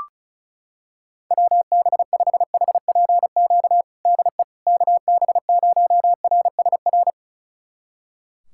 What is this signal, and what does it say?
Morse code, heard as a keyed tone, about 36 wpm, 700 Hz. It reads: WB5HPQ DE KB0RSR
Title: USA Callsign DE Callsign - 6 Characters - 35wpm